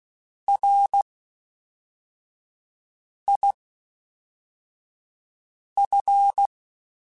Morse code, heard as a keyed tone, about 16 wpm, 800 Hz. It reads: R I F